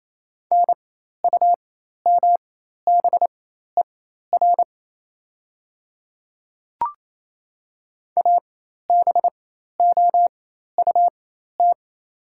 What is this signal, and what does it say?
Morse code, heard as a keyed tone, about 28 wpm, spaced as for 13 wpm, 700 Hz. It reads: NUMBER ABOUT
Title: 4x Character Spacing - Top 500 Words - 28wpm